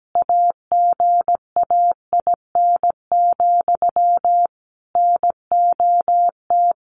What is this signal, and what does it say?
Morse code, heard as a keyed tone, about 17 wpm, 700 Hz. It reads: AGAIN, NOT
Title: A Sound Of Thunder 17wpm